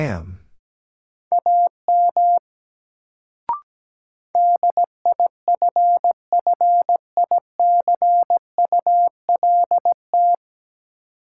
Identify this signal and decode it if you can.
Morse code, heard as a keyed tone, about 17 wpm, 700 Hz. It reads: AM DIFFICULT
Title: Sentences from Top 600 Words - New Word Drill - 17wpm